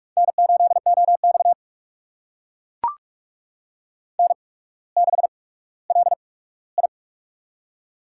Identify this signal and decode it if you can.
Morse code, heard as a keyed tone, about 44 wpm, spaced as for 19 wpm, 700 Hz. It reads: N8OX N6LI
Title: USA Call Signs - 4 Characters - 8x Character Spacing - 45wpm